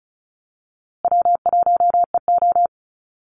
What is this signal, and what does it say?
Morse code, heard as a keyed tone, about 34 wpm, 700 Hz. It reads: W1EO
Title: USA Call Signs Repeated 3x 35wpm